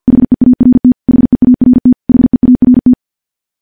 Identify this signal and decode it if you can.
Morse code, heard as a keyed tone, about 49 wpm, 250 Hz. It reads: HEART HEART HEART
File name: morse / morse-training-50WPM